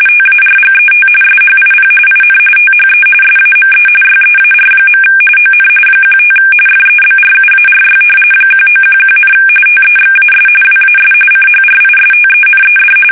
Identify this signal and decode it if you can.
RTTY, about 45.45 baud. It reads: TARGET SHIP IN THE SAME YEAR. ON 15 FEBRUARY 1911, TEXAS WAS CHRISTENED
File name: fsk850 / battleships